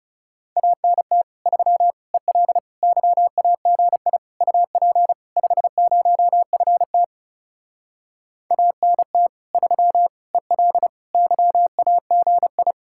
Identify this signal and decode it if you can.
Morse code, heard as a keyed tone, about 37 wpm, 700 Hz. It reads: ANT 3 EL YAGI UP 50FT ANT 3 EL YAGI
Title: Speed-Racing Common QSO Phrases - 30wpm